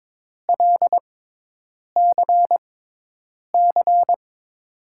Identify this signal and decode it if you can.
Morse code, heard as a keyed tone, about 22 wpm, spaced as for 8 wpm, 700 Hz. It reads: LCC
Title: USA Call Signs - 6x Character Spacing - 22wpm